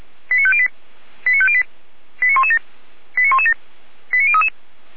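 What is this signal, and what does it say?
Five-tone selective call (ZVEI), heard as quick runs of five tones, tones of about 70 ms. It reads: 89508 89508 89117 89117 89033
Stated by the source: ZVEI five-tone selective call received via SDR.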